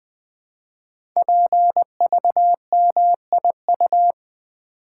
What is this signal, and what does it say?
Morse code, heard as a keyed tone, about 20 wpm, 700 Hz. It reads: PVMIU
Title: Code Groups - 5 Letter:Numbers - 20wpm